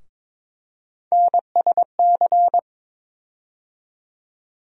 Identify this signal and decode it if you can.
Morse code, heard as a keyed tone, about 22 wpm, 700 Hz. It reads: NSC
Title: USA Call Sign Postfixes - 3 Letters 22wpm